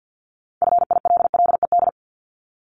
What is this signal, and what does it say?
Morse code, heard as a keyed tone, about 50 wpm, 700 Hz. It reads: FILLED
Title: Multi-letter Phonemes 50wpm